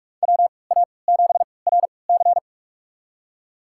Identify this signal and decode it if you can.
Morse code, heard as a keyed tone, about 46 wpm, spaced as for 25 wpm, 700 Hz. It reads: WA7RC
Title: USA Call Signs - 5 Characters - 3x Character Spacing - 45wpm